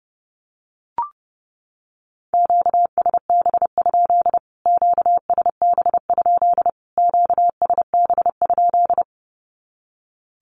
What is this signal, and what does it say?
Morse code, heard as a keyed tone, about 30 wpm, 700 Hz. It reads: QSB? QSB? QSB?
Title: Q-Codes Repeated 3x 30wpm